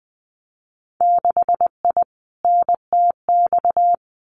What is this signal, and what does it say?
Morse code, heard as a keyed tone, about 20 wpm, 700 Hz. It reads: 6I NTX
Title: Winter Field Day Exchanges 20wpm